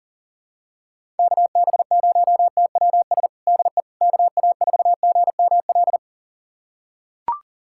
Morse code, heard as a keyed tone, about 37 wpm, 700 Hz. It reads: KB0TWS DE KA4GML
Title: USA Callsign DE Callsign - 6 Characters - 40wpm